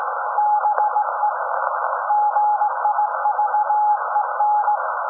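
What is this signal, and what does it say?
Morse code, heard as a keyed tone, about 18 wpm, 840 Hz, with noise. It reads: BE GIVEN
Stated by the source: SAQ Grimeton Radio, Alexanderson 200 kW alternator on 17.2 kHz, received in Potsdam with a very strong signal on World Radio Day.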